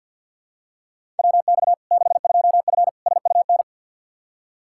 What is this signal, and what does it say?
Morse code, heard as a keyed tone, about 52 wpm, 700 Hz. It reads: WX 62F SUN
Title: Practice QSO - Exchange 2 - Info - 50wpm